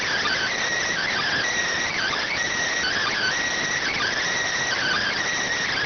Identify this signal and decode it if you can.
Emergency Alert System (SAME) tones (header bursts), heard as harsh buzzing bursts. It is heard from 0.9 to 5.2 s.